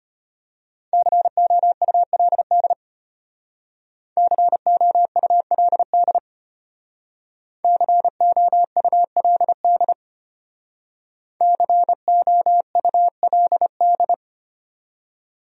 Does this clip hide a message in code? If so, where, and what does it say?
Yes: Morse code from 0.9 to 14.1 s, heard as a keyed tone, about 33 wpm, 700 Hz. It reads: COULD COULD COULD COULD